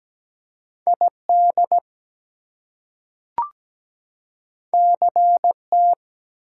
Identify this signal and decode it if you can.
Morse code, heard as a keyed tone, about 17 wpm, 700 Hz. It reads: ID CT
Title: US States and Canadian Provinces 17wpm